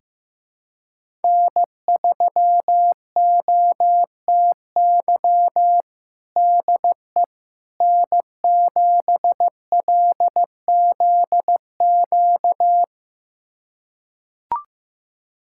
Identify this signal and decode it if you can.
Morse code, heard as a keyed tone, about 15 wpm, 700 Hz. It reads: N3OTY DE N7LZQ
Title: USA Callsign DE Callsign - 5 Characters - 15wpm